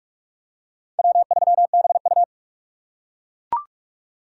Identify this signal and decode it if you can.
Morse code, heard as a keyed tone, about 42 wpm, 700 Hz. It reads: W3BU
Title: USA Call Signs - 4 Characters - 3x Character Spacing - 45wpm